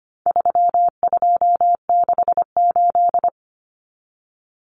Morse code, heard as a keyed tone, about 25 wpm, 700 Hz. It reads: 3268
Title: Numbers - 4-Digit 25wpm